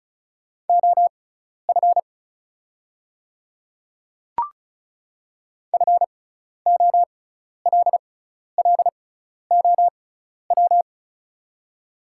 Morse code, heard as a keyed tone, about 36 wpm, spaced as for 12 wpm, 690 Hz. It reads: OF FOLLOW
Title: 6x Character Spacing - Top 300 Words - 35wpm